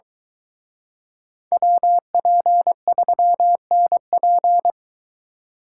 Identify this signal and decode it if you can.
Morse code, heard as a keyed tone, about 23 wpm, 700 Hz. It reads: WP3NP